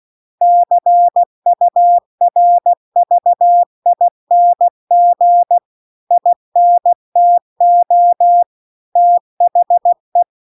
Morse code, heard as a keyed tone, about 16 wpm, 700 Hz. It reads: CURVING INTO THE
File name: CW-The-Big-Twoheaded-River-2-16WPM